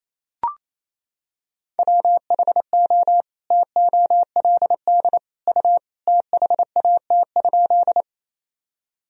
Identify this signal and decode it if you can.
Morse code, heard as a keyed tone, about 28 wpm, 700 Hz. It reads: WHO TOLD U THAT?
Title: Sentences from Top 400 Words and Abbreviations 28wpm